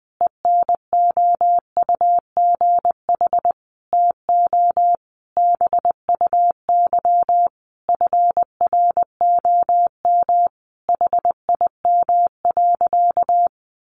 Morse code, heard as a keyed tone, about 19 wpm, 700 Hz. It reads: ENOUGH TO BUY FROM HIM.
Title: Sentences from Top 1200 Words 20wpm